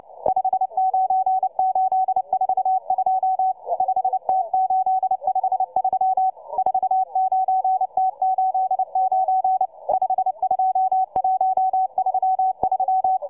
Morse code, heard as a keyed tone, about 29 wpm, 760 Hz, with radio noise. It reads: 59842 5T853 49T89 52133